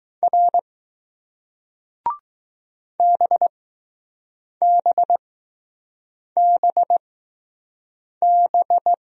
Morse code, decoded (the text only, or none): R BBBB